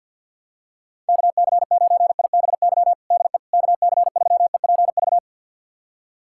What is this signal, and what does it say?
Morse code, heard as a keyed tone, about 48 wpm, 700 Hz. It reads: KC9IBY DE KC3EPV
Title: USA Callsign DE Callsign - 6 Characters - 50wpm